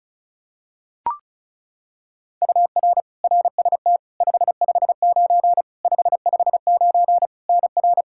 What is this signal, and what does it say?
Morse code, heard as a keyed tone, about 36 wpm, 700 Hz. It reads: UR RST 559 559 NR
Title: Practice QSO - Exchange 1 - Beginning - 35wpm